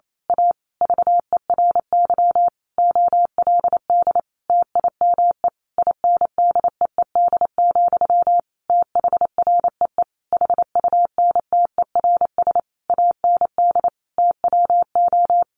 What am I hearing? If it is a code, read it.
Morse code, heard as a keyed tone, about 28 wpm, 700 Hz. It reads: A VERY OLD TIME INDEED, THREE HUNTERS AND TWO
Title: A Sound Of Thunder 28wpm